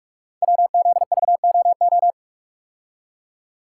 Morse code, heard as a keyed tone, about 44 wpm, 700 Hz. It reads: WZVOO
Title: Code Groups - 5 Letter:Numbers - 45wpm